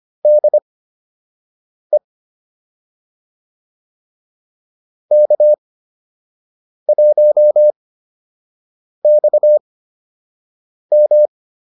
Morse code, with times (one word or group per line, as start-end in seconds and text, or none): 0.3-2.0 DE
5.1-11.3 K1XM